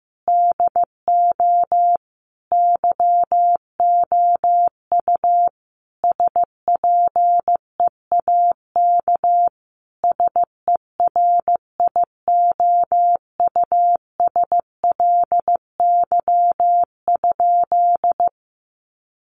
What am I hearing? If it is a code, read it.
Morse code, heard as a keyed tone, about 15 wpm, 700 Hz. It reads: DO YOU SPEAK SERIOUSLY?
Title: Sentences from Top 1200 Words 15wpm